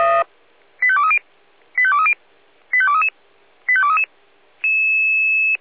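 Five-tone selective call (ZVEI), heard as quick runs of five tones, tones of about 70 ms. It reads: 85230 85230 85233 85233
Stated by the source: ZVEI five-tone call 85852 received by SDR.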